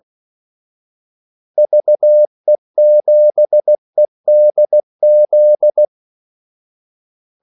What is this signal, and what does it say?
Morse code, heard as a keyed tone, about 16 wpm, 600 Hz. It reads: VE7EDZ
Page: Kaldesignaler 16 wpm